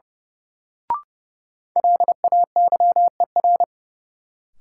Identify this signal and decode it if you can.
Morse code, heard as a keyed tone, about 30 wpm, 700 Hz. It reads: LAYER
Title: Sentences from Top 1600 Words - New Word Drill Alt - 30wpm